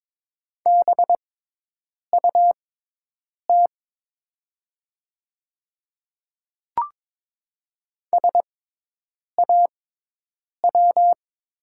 Morse code, heard as a keyed tone, about 22 wpm, spaced as for 8 wpm, 700 Hz. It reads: BUT SAW
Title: 6x Character Spacing - Top 300 Words - 22wpm